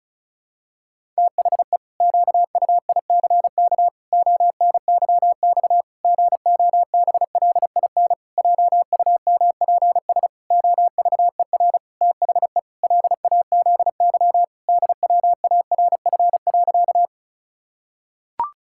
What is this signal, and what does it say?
Morse code, heard as a keyed tone, about 36 wpm, 700 Hz. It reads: THE QUICK ONYX GOBLIN JUMPS OVER THE LAZY DWARF.
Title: Pangram Challenge v2 - 35wpm